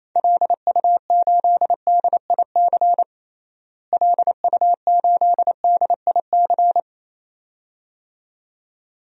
Morse code, heard as a keyed tone, about 27 wpm, 700 Hz. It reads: LU8DIC LU8DIC
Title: DXCC-100 Call Signs - 6 Characters - Repeated 2x - 28wpm